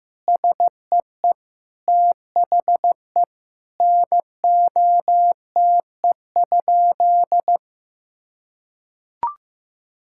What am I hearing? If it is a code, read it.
Morse code, heard as a keyed tone, about 15 wpm, 700 Hz. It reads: SEE THE NOTE?
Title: Sentences from Top 1600 Words 15wpm